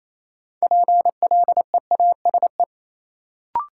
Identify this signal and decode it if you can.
Morse code, heard as a keyed tone, about 28 wpm, 700 Hz. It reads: PLEASE
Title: Sentences from Top 1500 Words - New Word Drill - 28wpm